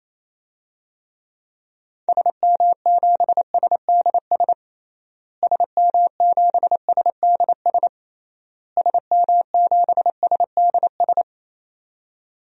Morse code, heard as a keyed tone, about 28 wpm, 700 Hz. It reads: SM7SDS SM7SDS SM7SDS
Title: DXCC-100 Call Signs - 6 Characters - Repeated 3x - 28wpm